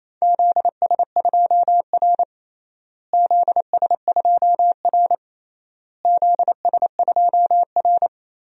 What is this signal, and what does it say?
Morse code, heard as a keyed tone, about 28 wpm, 700 Hz. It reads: ZS2R ZS2R ZS2R
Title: DXCC-100 Call Signs - 4 Characters - Repeated 3x - 28wpm